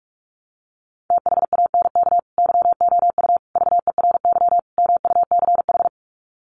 Morse code, heard as a keyed tone, about 44 wpm, 700 Hz. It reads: THANK YOU VERY MUCH
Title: Sets of 4 Words 45wpm